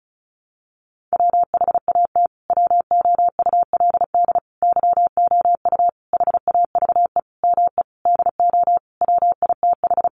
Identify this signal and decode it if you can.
Morse code, heard as a keyed tone, about 34 wpm, 700 Hz. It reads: WHAT WOULD YOU HAVE ME DO WITH
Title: Sentences from Top 300 Words 35wpm